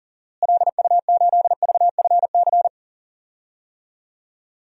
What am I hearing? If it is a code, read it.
Morse code, heard as a keyed tone, about 40 wpm, 700 Hz. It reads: LU8VFC
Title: DXCC-100 Call Signs - 6 Characters - Repeated 3x - 40wpm